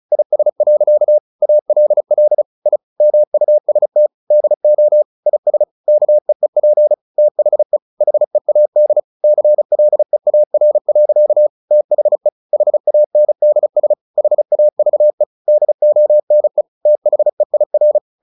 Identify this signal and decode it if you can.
Morse code, heard as a keyed tone, about 34 wpm, 600 Hz. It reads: IS. ALL I MUST DO IS KEEP THE HEAD CLEAR. THE HANDS HAVE DONE THEIR
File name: the old man and the sea 2 35wpm